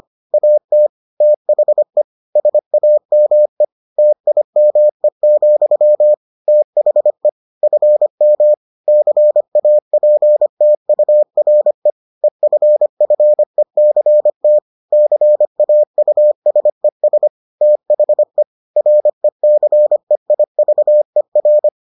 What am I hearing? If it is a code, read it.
Morse code, heard as a keyed tone, about 25 wpm, 600 Hz. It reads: AT THE SAME TIME, THE FM CAPTURE EFFECT CAUSES THE RECEIVER